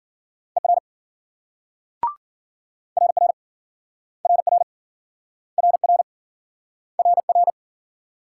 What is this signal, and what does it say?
Morse code, heard as a keyed tone, about 47 wpm, 700 Hz. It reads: ER RR RR RR RR
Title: Speed-Racing Multi-letter Phonemes - 40wpm